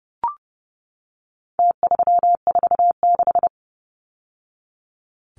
Morse code, heard as a keyed tone, about 30 wpm, 700 Hz. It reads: T346
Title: Numbers - Contest Style - 4-Digit 30wpm